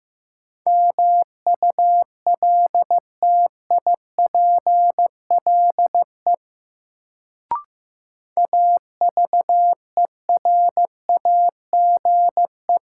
Morse code, heard as a keyed tone, about 16 wpm, 700 Hz. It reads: MULTIPLE AVERAGE
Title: Sets of 5 Words - New Word Drill - 15wpm